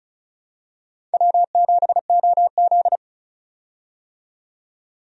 Morse code, heard as a keyed tone, about 34 wpm, 700 Hz. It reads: W7OZ
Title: USA Call Signs - 4 Characters - 35wpm